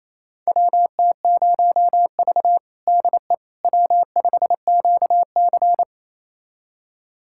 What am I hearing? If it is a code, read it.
Morse code, heard as a keyed tone, about 28 wpm, 700 Hz. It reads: WT0V DE W5QC
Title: USA Callsign DE Callsign - 4 Characters - 28wpm